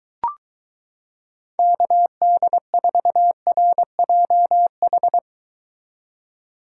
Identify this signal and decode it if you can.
Morse code, heard as a keyed tone, about 23 wpm, 700 Hz. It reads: KD4RJH